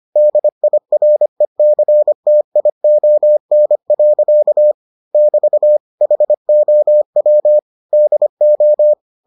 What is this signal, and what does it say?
Morse code, heard as a keyed tone, about 25 wpm, 600 Hz. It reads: DIRECTION. <BT> HOW DO